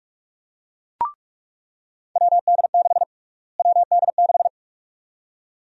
Morse code, heard as a keyed tone, about 47 wpm, 700 Hz. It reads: WD6 WD6
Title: USA Call Sign Prefixes - Repeated 2x - 45wpm